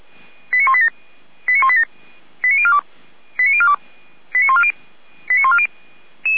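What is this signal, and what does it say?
Five-tone selective call (ZVEI), heard as quick runs of five tones, tones of about 70 ms. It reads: 89187 89187 89042 89042 89140 89140
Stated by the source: Five-tone ZVEI selective call 89097, received via SDR.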